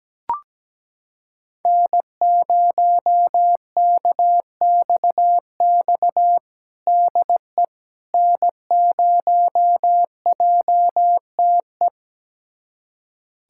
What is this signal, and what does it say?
Morse code, heard as a keyed tone, about 17 wpm, 700 Hz. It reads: N0KXX DE N0JTE
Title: USA Callsign DE Callsign - 5 Characters - 17wpm